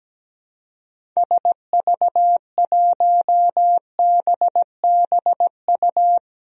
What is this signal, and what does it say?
Morse code, heard as a keyed tone, about 17 wpm, 700 Hz. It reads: SV1BBU
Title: DXCC-100 Call Signs - 6 Characters - Repeated 2x - 17wpm